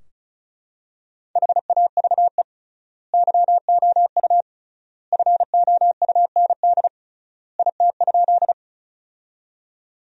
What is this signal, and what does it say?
Morse code, heard as a keyed tone, about 36 wpm, 700 Hz. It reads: HAVE YOU FOUND IT?
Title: 3x Word-Spacing - Sentences from Top 300 Words - 35wpm